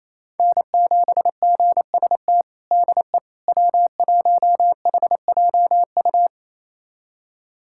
Morse code, heard as a keyed tone, about 28 wpm, 700 Hz. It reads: N7GST DE W1HJU
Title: USA Callsign DE Callsign - 5 Characters - 28wpm